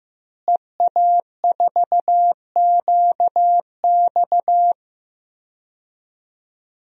Morse code, heard as a keyed tone, about 15 wpm, 700 Hz. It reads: EA4QX
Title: DXCC-100 Call Signs - 5 Characters - Repeated 3x - 15wpm